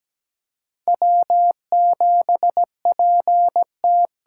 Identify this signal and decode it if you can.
Morse code, heard as a keyed tone, about 17 wpm, 700 Hz. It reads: W7PT
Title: USA Call Signs - 5 Characters - 8x Character Spacing - 17wpm